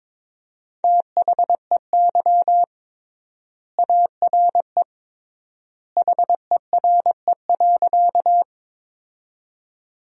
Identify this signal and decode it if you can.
Morse code, heard as a keyed tone, about 21 wpm, 700 Hz. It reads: THEY ARE HERE.
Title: 3x Word-Spacing - Sentences from Top 200 Words - 22wpm